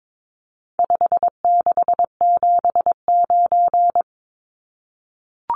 Morse code, heard as a keyed tone, about 22 wpm, 700 Hz. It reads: 5679
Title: Numbers - 4-Digit 22wpm